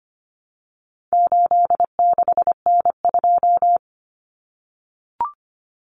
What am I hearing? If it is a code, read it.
Morse code, heard as a keyed tone, about 25 wpm, 700 Hz. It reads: 86N2
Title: Common Cut Numbers - 4-Digit 25wpm